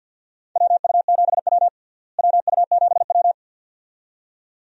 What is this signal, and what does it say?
Morse code, heard as a keyed tone, about 53 wpm, 700 Hz. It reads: WU7W WU7W
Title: USA Call Signs - 4 Characters - Repeated 2x - 50wpm